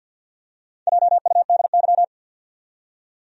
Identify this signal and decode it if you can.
Morse code, heard as a keyed tone, about 46 wpm, 700 Hz. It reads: JUDY
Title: USA Common Names - 50wpm